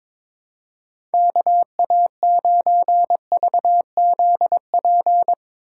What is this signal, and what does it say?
Morse code, heard as a keyed tone, about 22 wpm, 700 Hz. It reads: KA9VZP
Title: USA Call Signs - 6 Characters - 8x Character Spacing - 22wpm